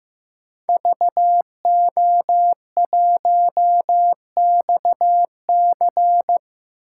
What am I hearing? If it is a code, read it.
Morse code, heard as a keyed tone, about 15 wpm, 700 Hz. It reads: VO1XC
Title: K1USN SST QSO Practice - 15wpm